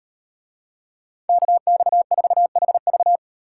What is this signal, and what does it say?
Morse code, heard as a keyed tone, about 38 wpm, 700 Hz. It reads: KX4HV